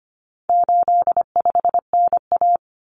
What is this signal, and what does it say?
Morse code, heard as a keyed tone, about 25 wpm, 700 Hz. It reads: 85NA
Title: Common Cut Numbers - 4-Digit 25wpm